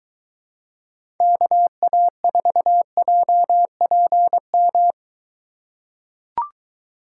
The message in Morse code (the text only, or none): KA4JPM